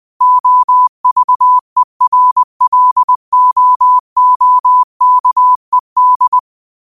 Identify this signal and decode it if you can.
Morse code, heard as a keyed tone, about 20 wpm, 1000 Hz. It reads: OVERLOOKED
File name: qotdfiles / 20wpm